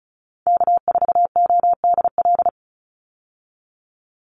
Morse code, heard as a keyed tone, about 34 wpm, 700 Hz. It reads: K4ODL
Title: Call Signs Repeated 3x - 35wpm